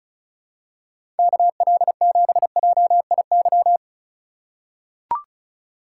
Morse code, heard as a keyed tone, about 36 wpm, 700 Hz. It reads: KL7JIY
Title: USA Call Signs - 6 Characters - 3x Character Spacing - 35wpm